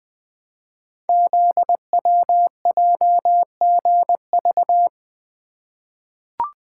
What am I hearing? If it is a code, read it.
Morse code, heard as a keyed tone, about 20 wpm, 700 Hz. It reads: ZWJGV
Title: Code Groups - 5 Letters - 20wpm